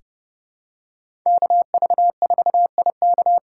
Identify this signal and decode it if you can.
Morse code, heard as a keyed tone, about 30 wpm, 700 Hz. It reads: KV4IK
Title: USA Call Signs - 5 Characters - 3x Character Spacing - 30wpm